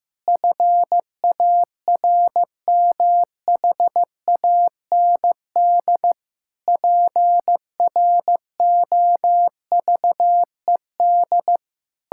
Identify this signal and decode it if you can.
Morse code, heard as a keyed tone, about 15 wpm, 700 Hz. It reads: FARMHAND PROVED
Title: Pangram Challenge v2 - 15wpm